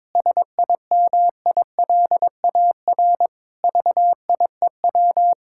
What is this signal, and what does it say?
Morse code, heard as a keyed tone, about 22 wpm, 700 Hz. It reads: SIMILAR VIEW
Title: Sentences from Top 1000 Words - Review Part 6 - 22wpm